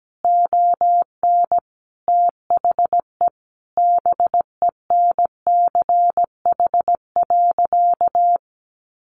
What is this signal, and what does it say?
Morse code, heard as a keyed tone, about 17 wpm, 700 Hz. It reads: ON THE BENCH.
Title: Sentences from Top 1400 Words 17wpm